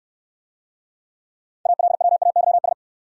Morse code, heard as a keyed tone, about 69 wpm, 690 Hz. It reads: ALWAYS